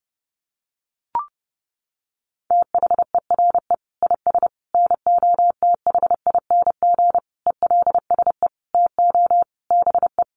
Morse code, heard as a keyed tone, about 30 wpm, 700 Hz. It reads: THERE IS NOTHING ELSE TO BE
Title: Sentences from Top 800 Words 30wpm